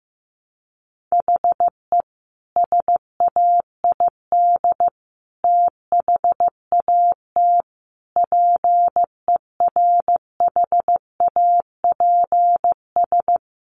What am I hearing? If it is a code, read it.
Morse code, heard as a keyed tone, about 14 wpm, 700 Hz. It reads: HE SAID THAT PERHAPS
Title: Sentences from Top 1200 Words 15wpm